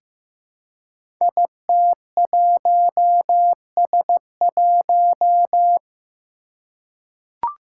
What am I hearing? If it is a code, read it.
Morse code, heard as a keyed tone, about 15 wpm, 700 Hz. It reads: IT1S1
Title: Five Challenging Letter-Number Combinations 15wpm